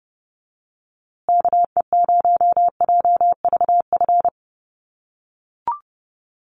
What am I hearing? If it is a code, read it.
Morse code, heard as a keyed tone, about 29 wpm, 700 Hz. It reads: KE0JVF
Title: USA Call Signs Repeated 2x 30wpm